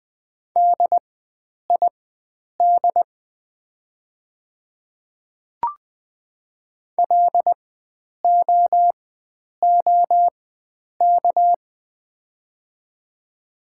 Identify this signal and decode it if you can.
Morse code, heard as a keyed tone, about 20 wpm, spaced as for 9 wpm, 700 Hz. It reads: DID LOOK
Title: 4x Character Spacing - Top 500 Words - 20wpm